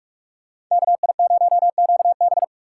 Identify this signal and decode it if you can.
Morse code, heard as a keyed tone, about 47 wpm, 700 Hz. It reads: KI0QB
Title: USA Call Signs - 5 Characters - 3x Character Spacing - 45wpm